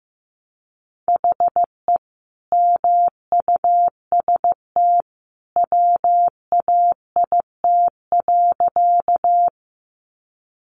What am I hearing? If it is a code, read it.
Morse code, heard as a keyed tone, about 15 wpm, 700 Hz. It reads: HE MUST WAIT.
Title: Sentences from Top 500 Words 15wpm